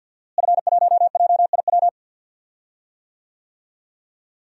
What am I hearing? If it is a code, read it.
Morse code, heard as a keyed tone, about 49 wpm, 700 Hz. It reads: F1JIW
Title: DXCC-100 Call Signs - 5 Characters - Repeated 3x - 50wpm